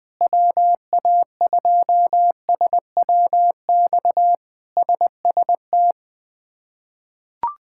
Morse code, heard as a keyed tone, about 20 wpm, 700 Hz. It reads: WA2SWX SST
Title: K1USN SST QSO Practice - 20wpm